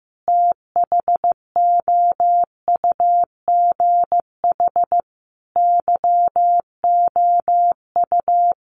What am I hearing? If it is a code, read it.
Morse code, heard as a keyed tone, about 15 wpm, 700 Hz. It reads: THOUGH YOU
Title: Sentences from Top 1500 Words 15wpm